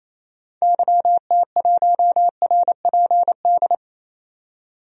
Morse code, heard as a keyed tone, about 28 wpm, 700 Hz. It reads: YT1RPD
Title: DXCC-100 Call Signs - 6 Characters - 28wpm